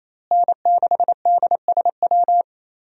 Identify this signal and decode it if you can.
Morse code, heard as a keyed tone, about 27 wpm, 700 Hz. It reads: N6DSW
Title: USA Call Signs - 5 Characters - Repeated 3x - 28wpm